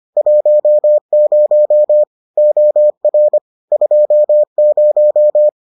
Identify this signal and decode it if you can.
Morse code, heard as a keyed tone, about 25 wpm, 600 Hz. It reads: 10 OR 20